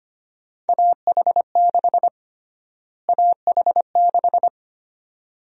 Morse code, heard as a keyed tone, about 25 wpm, 700 Hz. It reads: AH6 AH6
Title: USA Call Sign Prefixes - Repeated 2x - 25wpm